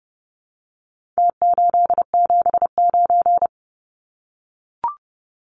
Morse code, heard as a keyed tone, about 30 wpm, 700 Hz. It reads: T879
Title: Numbers - Contest Style - 4-Digit 30wpm